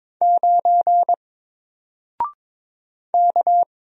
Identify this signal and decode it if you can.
Morse code, heard as a keyed tone, about 22 wpm, 700 Hz. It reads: 9 K
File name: Single-Letter-Number-22wpm